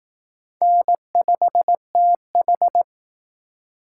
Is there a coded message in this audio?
Yes: Morse code from 0.6 to 2.8 s, heard as a keyed tone, about 20 wpm, 700 Hz. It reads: N5TH